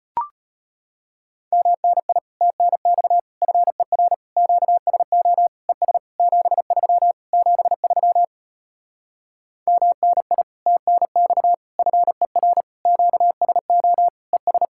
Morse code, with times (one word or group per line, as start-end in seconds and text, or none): 1.5-2.2 MNI
2.4-3.2 TNX
3.4-4.1 FER
4.4-5.5 QSO
5.7-6.0 ES
6.2-7.1 73
7.3-8.2 73
9.7-10.4 MNI
10.7-11.5 TNX
11.8-12.6 FER
12.9-14.1 QSO
14.3-14.7 ES